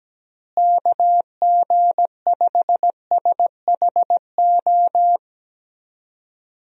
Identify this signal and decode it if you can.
Morse code, heard as a keyed tone, about 17 wpm, 700 Hz. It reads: KG5SHO
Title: USA Call Signs - 6 Characters - Repeated 2x - 17wpm